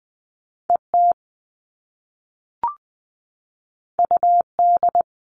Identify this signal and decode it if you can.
Morse code, heard as a keyed tone, about 20 wpm, 700 Hz. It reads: ET UD
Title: All Cut Numbers - 2-Digit 20wpm